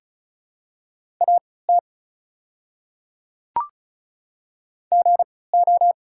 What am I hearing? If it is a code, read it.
Morse code, heard as a keyed tone, about 36 wpm, spaced as for 20 wpm, 690 Hz. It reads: AT GO
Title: 3x Character Spacing - Top 100 Words - 35wpm